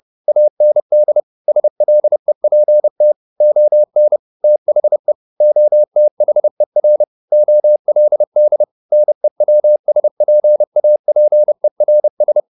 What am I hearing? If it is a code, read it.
Morse code, heard as a keyed tone, about 30 wpm, 600 Hz. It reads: AND SLEPT ON THE OTHER OLD NEWSPAPERS
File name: the old man and the sea / the old man and the sea 1 30wpm